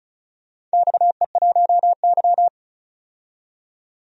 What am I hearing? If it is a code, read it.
Morse code, heard as a keyed tone, about 36 wpm, 700 Hz. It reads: XE1Y
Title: DXCC-100 Call Signs - 4 Characters - 35wpm